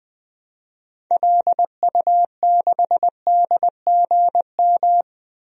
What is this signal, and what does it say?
Morse code, heard as a keyed tone, about 20 wpm, 700 Hz. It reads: LU6DGM
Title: DXCC-100 Call Signs - 6 Characters - 20wpm